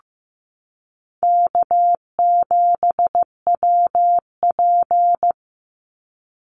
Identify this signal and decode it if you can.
Morse code, heard as a keyed tone, about 15 wpm, 700 Hz. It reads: K7WP
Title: Easy Call Signs Repeated 3x - 15wpm